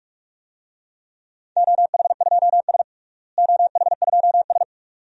Morse code, heard as a keyed tone, about 41 wpm, 700 Hz. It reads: OH2S OH2S
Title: DXCC-100 Call Signs - 4 Characters - Repeated 2x - 45wpm